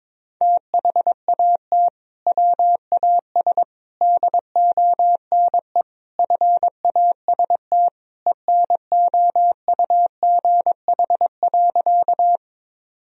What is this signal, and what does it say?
Morse code, heard as a keyed tone, about 22 wpm, 700 Hz. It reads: THAT WAS DONE FAST ENOUGH.
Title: Sentences from Top 1000 Words - Review Part 2 - 22wpm